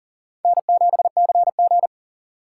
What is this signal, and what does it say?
Morse code, heard as a keyed tone, about 37 wpm, 700 Hz. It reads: N7CG
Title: USA Call Signs - 4 Characters - 4x Character Spacing - 40wpm